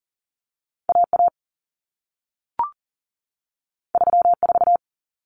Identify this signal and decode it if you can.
Morse code, heard as a keyed tone, about 40 wpm, 700 Hz. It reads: AA 34
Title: Common Cut Numbers - 2-Digit 40wpm